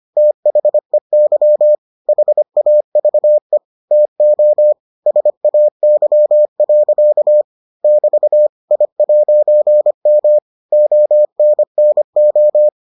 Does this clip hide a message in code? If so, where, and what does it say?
Yes: Morse code from 0.2 to 12.7 s, heard as a keyed tone, about 25 wpm, 600 Hz. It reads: THEY HAVE TO SAY. <BT> I'M ONNO